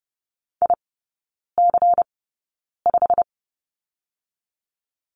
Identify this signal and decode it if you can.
Morse code, heard as a keyed tone, about 30 wpm, spaced as for 9 wpm, 700 Hz. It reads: IC5
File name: Lesson 16e - Letter c - Three Characters 30wpm